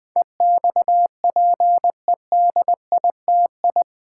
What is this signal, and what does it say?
Morse code, heard as a keyed tone, about 20 wpm, 700 Hz. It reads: EXPEDITI
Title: Build A Word - 4 Letters At A Time - 10-Letter Words - 20wpm